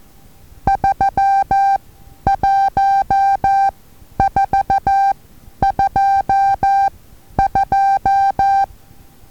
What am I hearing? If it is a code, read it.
Morse code, heard as a keyed tone, about 14 wpm, spaced as for 10 wpm, 790 Hz. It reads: 31422
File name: Taviro HA6VA A